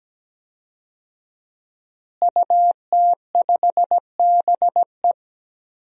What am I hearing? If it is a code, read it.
Morse code, heard as a keyed tone, about 17 wpm, 700 Hz. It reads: UT5BE
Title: DXCC-100 Call Signs - 5 Characters - Repeated 2x - 17wpm